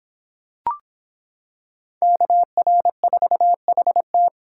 Morse code, heard as a keyed tone, about 25 wpm, 700 Hz. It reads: KR4HT